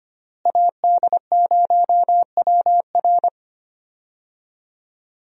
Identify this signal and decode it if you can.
Morse code, heard as a keyed tone, about 25 wpm, 700 Hz. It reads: AD0WR
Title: USA Call Signs - 5 Characters - Repeated 3x - 25wpm